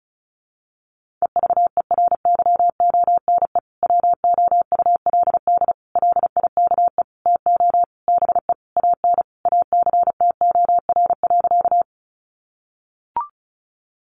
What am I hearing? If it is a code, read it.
Morse code, heard as a keyed tone, about 34 wpm, 700 Hz. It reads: EVERYONE WOULD LIKE TO BE AN ACTOR.
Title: Sentences from Top 1500 Words 35wpm